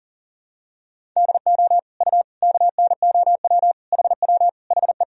Morse code, heard as a keyed tone, about 40 wpm, 700 Hz. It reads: DO U KNOW HW HE
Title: Sentences from Top 300 Words and Abbreviations 40wpm